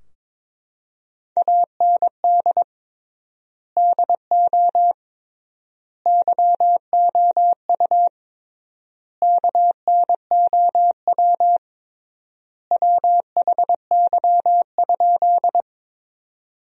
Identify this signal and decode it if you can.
Morse code, heard as a keyed tone, about 22 wpm, 700 Hz. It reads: AND DO YOU KNOW WHY?
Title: 3x Word-Spacing - Sentences from Top 200 Words - 22wpm